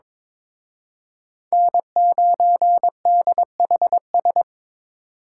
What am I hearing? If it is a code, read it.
Morse code, heard as a keyed tone, about 26 wpm, 700 Hz. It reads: N9DHS
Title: Speed Racing - USA Call Signs 22wpm